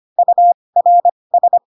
Morse code, heard as a keyed tone, about 25 wpm, spaced as for 20 wpm, 700 Hz. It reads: URS